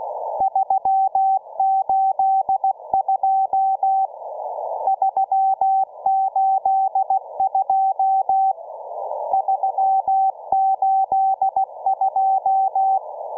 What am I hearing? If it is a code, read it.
Morse code, heard as a keyed tone, about 16 wpm, 770 Hz. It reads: 382 382 382